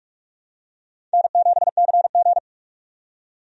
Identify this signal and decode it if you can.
Morse code, heard as a keyed tone, about 43 wpm, 700 Hz. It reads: N7CG
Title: USA Call Signs - 4 Characters - 45wpm